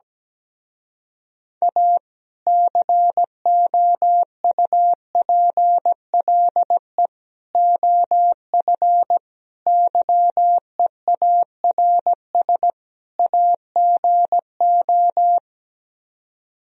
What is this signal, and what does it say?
Morse code, heard as a keyed tone, about 17 wpm, 700 Hz. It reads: A COUPLE OF YEARS AGO
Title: Speed-Racing - Sets of 5 Words 17wpm